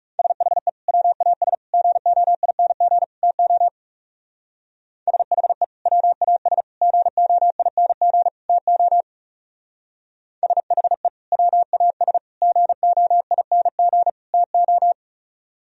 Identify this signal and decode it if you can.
Morse code, heard as a keyed tone, about 35 wpm, 700 Hz. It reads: SHE WAS GOING TO SHE WAS GOING TO SHE WAS GOING TO
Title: Speed-Racing - Sets of 4 Words 30wpm